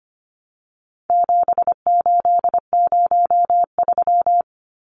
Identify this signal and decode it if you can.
Morse code, heard as a keyed tone, about 25 wpm, 700 Hz. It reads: 7803
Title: Numbers - 4-Digit 25wpm